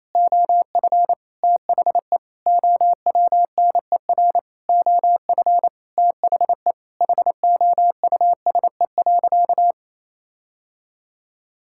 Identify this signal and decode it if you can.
Morse code, heard as a keyed tone, about 28 wpm, 700 Hz. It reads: OF THE OWNER OF THE HOUSE.
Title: Sentences from Top 1600 Words 28wpm